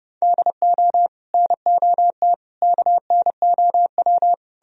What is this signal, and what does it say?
Morse code, heard as a keyed tone, about 40 wpm, spaced as for 35 wpm, 700 Hz. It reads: DO NOT KNOW
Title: Speed-Racing - Sets of 5 Words 30wpm